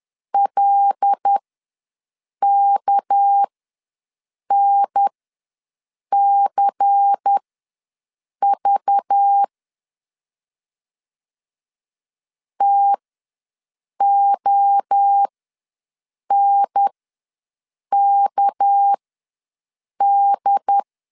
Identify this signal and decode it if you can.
Morse code, heard as a keyed tone, about 10 wpm, spaced as for 6 wpm, 800 Hz. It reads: LKNCV TONKD
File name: LEZIONI / 6 PKX0 / 6 PK ws 10wpm 3745